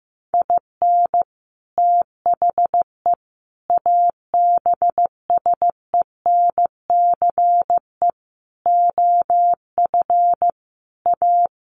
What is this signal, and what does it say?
Morse code, heard as a keyed tone, about 15 wpm, 700 Hz. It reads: IN THE ABSENCE OF A
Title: Sets of 5 Words Encore 15wpm